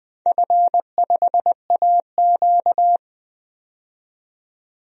Morse code, heard as a keyed tone, about 20 wpm, 700 Hz. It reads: F5AQ
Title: DXCC-100 Call Signs - 4 Characters - 20wpm